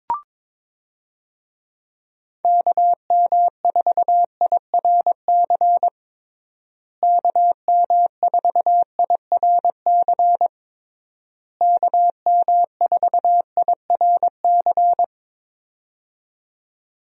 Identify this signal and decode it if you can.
Morse code, heard as a keyed tone, about 22 wpm, 700 Hz. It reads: KM4IRC KM4IRC KM4IRC
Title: USA Call Signs - 6 Characters - Repeated 3x - 22wpm